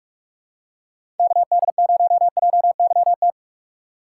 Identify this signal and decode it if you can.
Morse code, heard as a keyed tone, about 44 wpm, 700 Hz. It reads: KD0JYT
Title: USA Call Signs - 6 Characters - 4x Character Spacing - 45wpm